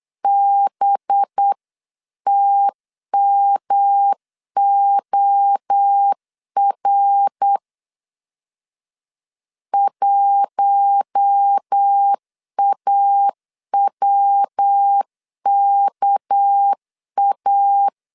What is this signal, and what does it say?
Morse code, heard as a keyed tone, about 8 wpm, 800 Hz. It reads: BTMOR 1AWKA
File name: 8wpm